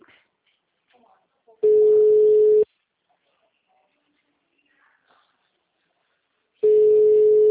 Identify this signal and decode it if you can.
Ringback tone (Europe, 425 Hz), with speech.